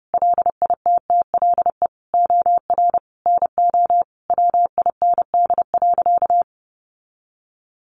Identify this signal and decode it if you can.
Morse code, heard as a keyed tone, about 30 wpm, 700 Hz. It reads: LITTLE OR NO WIND.
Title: Sentences from Top 1400 Words 30wpm